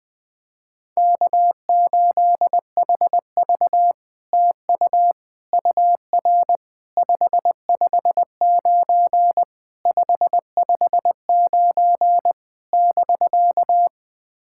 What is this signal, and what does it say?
Morse code, heard as a keyed tone, about 20 wpm, 700 Hz. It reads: K8HV TU UR 559 559 <BK>
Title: SOTA QSO Practice 20wpm